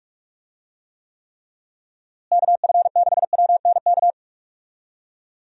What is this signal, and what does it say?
Morse code, heard as a keyed tone, about 46 wpm, 700 Hz. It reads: KF6WNK
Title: USA Call Signs - 6 Characters - Repeated 3x - 45wpm